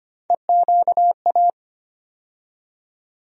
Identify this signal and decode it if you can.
Morse code, heard as a keyed tone, about 25 wpm, 700 Hz. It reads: EQA
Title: USA Call Sign Postfixes - 3 Letters 25wpm